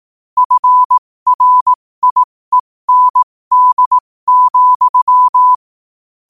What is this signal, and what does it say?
Morse code, heard as a keyed tone, about 18 wpm, spaced as for 15 wpm, 1000 Hz. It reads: FRIEND,
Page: New quotes every day in morse code at 15 Words per minute.